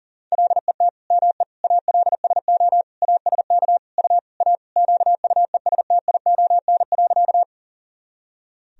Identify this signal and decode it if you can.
Morse code, heard as a keyed tone, about 40 wpm, 700 Hz. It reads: LET ME ALSO ASK U A QUESTION.
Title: Sentences from Top 400 Words and Abbreviations 40wpm